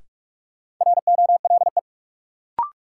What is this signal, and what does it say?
Morse code, heard as a keyed tone, about 46 wpm, 700 Hz. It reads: ROLE
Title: Sentences from Top 1500 Words - New Word Drill Alt - 45wpm